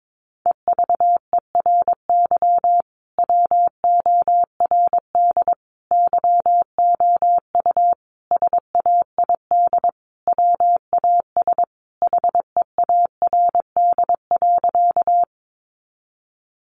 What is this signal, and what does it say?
Morse code, heard as a keyed tone, about 22 wpm, 700 Hz. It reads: EVERY WORD YOU SAID WAS HEARD.
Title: Sentences from Top 400 Words 22wpm